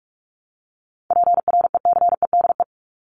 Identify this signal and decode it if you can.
Morse code, heard as a keyed tone, about 44 wpm, 700 Hz. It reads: PRECEDE
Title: ING Suffix - Top 100 Words with Base Words 45wpm